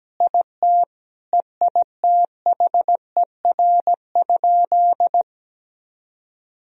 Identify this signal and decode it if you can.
Morse code, heard as a keyed tone, about 17 wpm, 700 Hz. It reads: IT EITHER?
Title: Sentences from Top 1000 Words - Review Part 4 - 17wpm